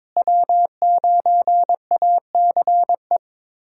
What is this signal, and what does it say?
Morse code, heard as a keyed tone, about 22 wpm, 700 Hz. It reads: W9ACE
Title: USA Callsign DE Callsign - 5 Characters - 22wpm